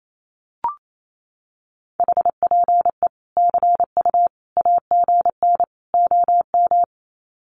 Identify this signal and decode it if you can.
Morse code, heard as a keyed tone, about 28 wpm, 700 Hz. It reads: HPE CU AGN OM
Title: Common QSO Phrases 28wpm